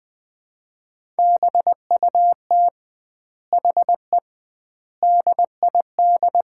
Morse code, heard as a keyed tone, about 20 wpm, 700 Hz. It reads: BUT HE DID
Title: 2x Word-Spacing - Sets of 4 Words - 20wpm